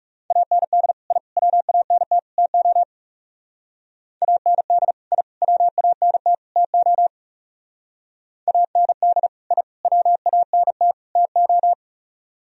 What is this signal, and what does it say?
Morse code, heard as a keyed tone, about 38 wpm, 700 Hz. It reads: AND I WANT TO AND I WANT TO AND I WANT TO